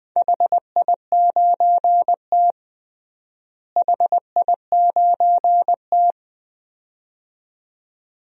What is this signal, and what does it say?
Morse code, heard as a keyed tone, about 20 wpm, 700 Hz. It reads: HI9T HI9T
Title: DXCC-100 Call Signs - 4 Characters - Repeated 2x - 20wpm